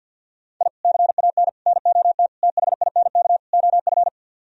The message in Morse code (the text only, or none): I CAN NOT THINK OF